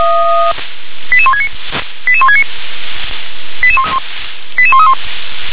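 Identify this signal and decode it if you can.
Five-tone selective call (ZVEI), heard as quick runs of five tones, tones of about 80 ms. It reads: 88169 88169 88131
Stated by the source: ZVEI five-tone selective call received by SDR.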